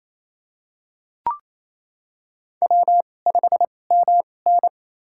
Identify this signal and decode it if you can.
Morse code, heard as a keyed tone, about 28 wpm, spaced as for 20 wpm, 700 Hz. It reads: W5MN
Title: USA Call Signs - 4 Characters - 2x Character Spacing - 28wpm